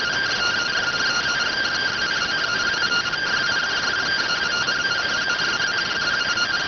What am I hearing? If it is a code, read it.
Two-tone teleprinter signal, about 170 Hz shift.